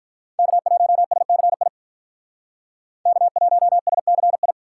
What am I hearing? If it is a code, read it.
Morse code, heard as a keyed tone, about 48 wpm, 700 Hz. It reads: K1SCI K1SCI